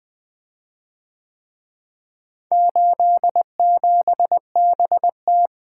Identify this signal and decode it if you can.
Morse code, heard as a keyed tone, about 25 wpm, 700 Hz. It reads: 87BT
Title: Speed-Racing - Four Letter-Number Combinations - 20wpm